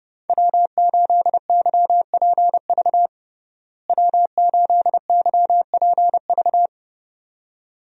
Morse code, heard as a keyed tone, about 30 wpm, 700 Hz. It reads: W8YPV W8YPV
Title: USA Call Signs - 5 Characters - Repeated 2x - 30wpm